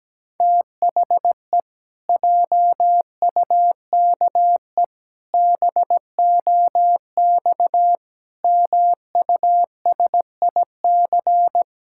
Morse code, heard as a keyed tone, about 17 wpm, 700 Hz. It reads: THE JUKE BOX MUSIC
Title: Pangram Challenge v2 - 17wpm